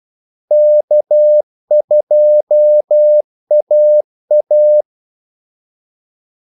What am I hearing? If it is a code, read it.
Morse code, heard as a keyed tone, about 12 wpm, 600 Hz. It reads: K2AA
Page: 2 ekstra ordmellemrum mellem kaldesignalerne